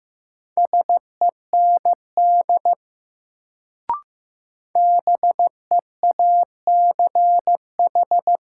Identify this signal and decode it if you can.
Morse code, heard as a keyed tone, about 15 wpm, 700 Hz. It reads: SEND BEACH
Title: Sentences from Top 1500 Words - New Word Drill - 15wpm